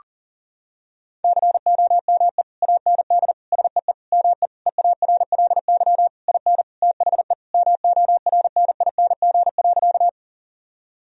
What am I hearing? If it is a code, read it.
Morse code, heard as a keyed tone, about 40 wpm, 700 Hz. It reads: COME AND SEE ME EARLY IN THE MORNING.
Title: Sentences from Top 1000 Words - Review Part 2 - 40wpm